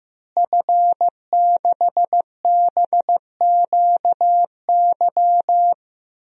Morse code, heard as a keyed tone, about 15 wpm, 700 Hz. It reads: F6BQY
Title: DXCC-100 Call Signs - 5 Characters - Repeated 3x - 15wpm